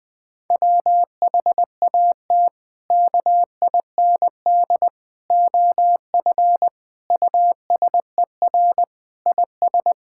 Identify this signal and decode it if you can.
Morse code, heard as a keyed tone, about 20 wpm, 700 Hz. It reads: WHAT KIND OF USER IS
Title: Sentences from Top 1000 Words - Review Part 5 - 20wpm